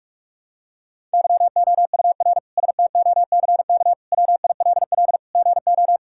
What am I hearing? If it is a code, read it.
Morse code, heard as a keyed tone, about 42 wpm, 700 Hz. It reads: YOUR STOCK WILL GO
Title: Sentences from Top 1000 Words - Review Part 5 - 45wpm